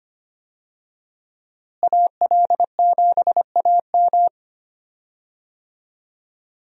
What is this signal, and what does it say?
Morse code, heard as a keyed tone, about 25 wpm, 700 Hz. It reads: AL7AM
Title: USA Call Signs - 5 Characters - Repeated 2x - 25wpm